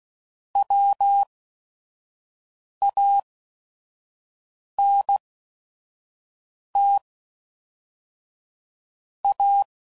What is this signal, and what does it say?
Morse code, heard as a keyed tone, about 16 wpm, spaced as for 5 wpm, 800 Hz. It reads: WANT A
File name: CW Code Groups @5WPM / Technical Subjects / SwPwrSup